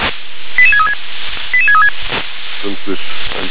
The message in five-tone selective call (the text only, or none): 88636 88636